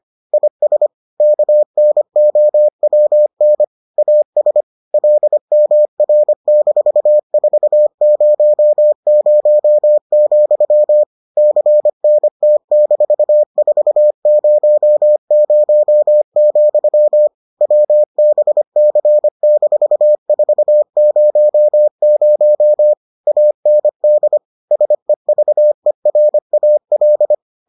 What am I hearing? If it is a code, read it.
Morse code, heard as a keyed tone, about 25 wpm, 600 Hz. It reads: IS KNOWN AS LMR-400, CNT-400, WBC-400 AND SEVERAL